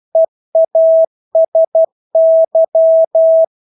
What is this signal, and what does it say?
Morse code, heard as a keyed tone, about 12 wpm, 650 Hz. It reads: EASY
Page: Engelske ord 12 wpm